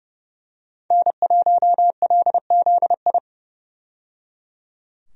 Morse code, heard as a keyed tone, about 30 wpm, 700 Hz. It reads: N1LZI
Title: USA Call Signs - 5 Characters - 30wpm